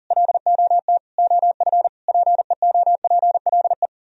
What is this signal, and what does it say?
Morse code, heard as a keyed tone, about 51 wpm, 700 Hz. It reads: LOT OF PEOPLE